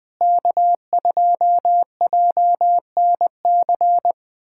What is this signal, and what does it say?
Morse code, heard as a keyed tone, about 20 wpm, 700 Hz. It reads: K2JNC
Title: USA Callsign DE Callsign - 5 Characters - 20wpm